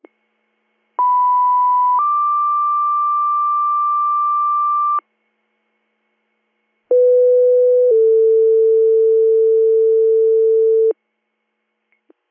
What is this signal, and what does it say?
Two-tone page, heard as a one-second tone, then a three-second tone, about 990 then 1180 Hz, about 500 then 450 Hz. It is heard from 1.0 to 10.9 s.